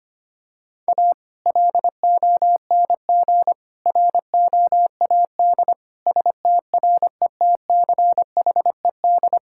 Morse code, heard as a keyed tone, about 25 wpm, 700 Hz. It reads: A LONG ROAD STRETCHED
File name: practice / 25wpm / Sentences with 2-Word Phrases - 25wpm